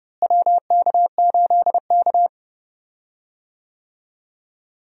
Morse code, heard as a keyed tone, about 30 wpm, 700 Hz. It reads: WK8K
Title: USA Call Signs - 4 Characters - Repeated 2x - 30wpm